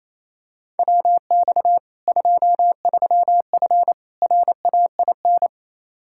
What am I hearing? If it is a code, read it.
Morse code, heard as a keyed tone, about 28 wpm, 700 Hz. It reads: WX 23F RAIN
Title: Practice QSO - Exchange 2 - Info - 28wpm